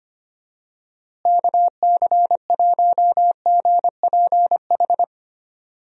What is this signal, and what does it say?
Morse code, heard as a keyed tone, about 25 wpm, 700 Hz. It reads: KC1GPH
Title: USA Call Signs - 6 Characters - Repeated 3x - 25wpm